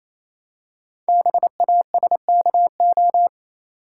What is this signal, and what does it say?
Morse code, heard as a keyed tone, about 27 wpm, 700 Hz. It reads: BASKO
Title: Code Groups - 5 Letters - 28wpm